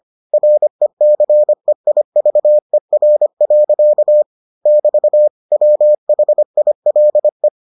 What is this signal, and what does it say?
Morse code, heard as a keyed tone, about 25 wpm, 600 Hz. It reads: RECEIVER. <BT> WHILE